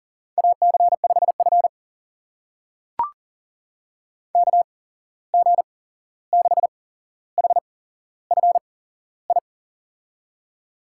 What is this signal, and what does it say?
Morse code, heard as a keyed tone, about 38 wpm, spaced as for 15 wpm, 690 Hz. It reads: AC5F KG6HFI
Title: USA Call Signs - 8x Character Spacing - 40wpm